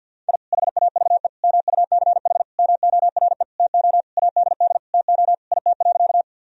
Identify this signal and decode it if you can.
Morse code, heard as a keyed tone, about 50 wpm, 700 Hz. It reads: I HAVE MUCH MORE TO ADD TO IT.
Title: Sentences from Top 1000 Words - Review Part 1 - 50wpm